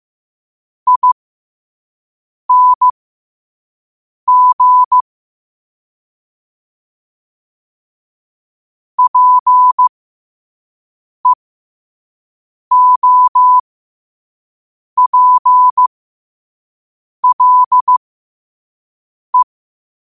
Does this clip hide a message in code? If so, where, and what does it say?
Yes: Morse code from 0.9 to 19.4 s, heard as a keyed tone, about 15 wpm, spaced as for 5 wpm, 1000 Hz. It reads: ING PEOPLE